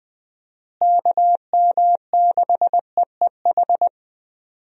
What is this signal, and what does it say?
Morse code, heard as a keyed tone, about 20 wpm, 700 Hz. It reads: KM6EEH
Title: USA Call Signs - 6 Characters - Repeated 2x - 20wpm